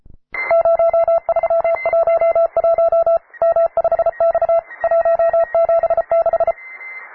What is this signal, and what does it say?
Morse code, heard as a keyed tone, about 33 wpm, 670 Hz, with radio noise. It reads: 0311 M5X 176
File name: wae / cw advanced / p3f cw